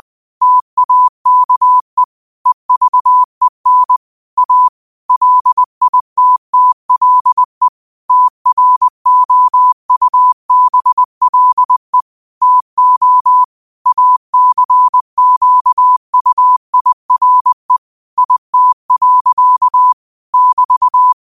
Morse code, heard as a keyed tone, about 20 wpm, 1000 Hz. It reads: TAKE EVEN A LITTLE TROUBLE TO ACQUIRE IT. <BT>
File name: qotdfiles / 20wpm